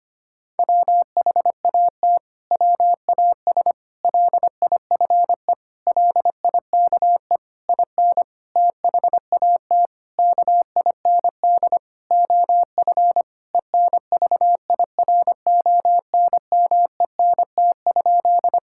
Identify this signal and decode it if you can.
Morse code, heard as a keyed tone, about 25 wpm, 700 Hz. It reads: WHAT WAS LIFE LIKE IN THAT KIND OF ENVIRONMENT?
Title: Sentences from Top 1000 Words - Review Part 5 - 25wpm